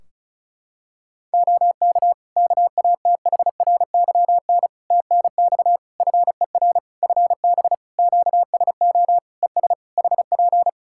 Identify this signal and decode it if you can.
Morse code, heard as a keyed tone, about 36 wpm, 700 Hz. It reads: OK KATHRYN TNX FER FB QSO ES HP
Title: Practice QSO - Exchange 4 - Ending - 35wpm